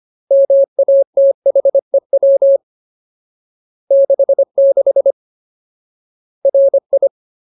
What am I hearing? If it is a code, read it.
Morse code, heard as a keyed tone, about 25 wpm, 550 Hz. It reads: MATHEW 66 RI
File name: Ultimate Word List / 25WPM Ultimate